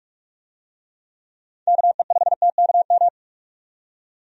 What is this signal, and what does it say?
Morse code, heard as a keyed tone, about 44 wpm, 700 Hz. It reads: KE5TKM
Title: USA Call Signs - 6 Characters - 45wpm